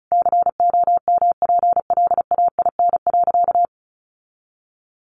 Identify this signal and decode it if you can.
Morse code, heard as a keyed tone, about 34 wpm, 700 Hz. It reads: COMPLAIN.
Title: Sentences from Top 1300 Words 35wpm